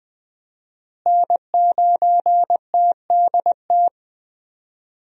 Morse code, heard as a keyed tone, about 20 wpm, 700 Hz. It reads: N9TDT
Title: USA Call Signs - 5 Characters - Repeated 3x - 20wpm